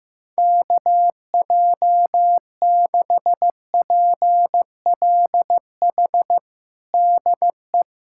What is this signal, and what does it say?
Morse code, heard as a keyed tone, about 15 wpm, 700 Hz. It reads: KJ6PLH DE
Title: USA Callsign DE Callsign - 6 Characters - 15wpm